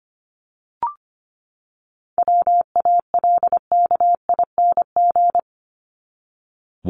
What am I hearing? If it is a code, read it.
Morse code, heard as a keyed tone, about 25 wpm, 700 Hz. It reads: WALKING
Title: ING Suffix - Top 100 Words with Base Words 25wpm